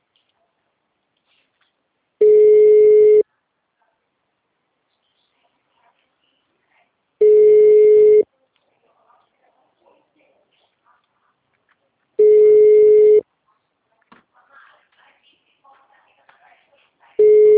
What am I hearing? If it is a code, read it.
Ringback tone (Europe, 425 Hz), with speech.
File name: out- -2002- - - .168